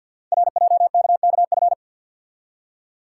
Morse code, heard as a keyed tone, about 51 wpm, 690 Hz. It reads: RJKKF
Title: Code Groups - 5 Letters - 50wpm